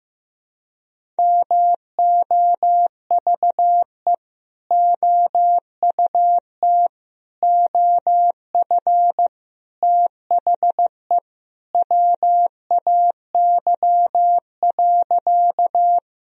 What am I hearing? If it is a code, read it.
Morse code, heard as a keyed tone, about 15 wpm, 700 Hz. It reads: MOVE OUT OF THE WAY.
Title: Sentences from Top 1000 Words - Review Part 1 - 15wpm